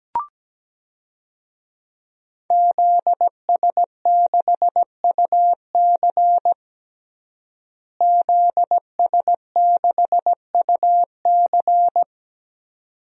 Morse code, heard as a keyed tone, about 17 wpm, 700 Hz. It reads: ZS6UC ZS6UC
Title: DXCC-100 Call Signs - 5 Characters - Repeated 2x - 17wpm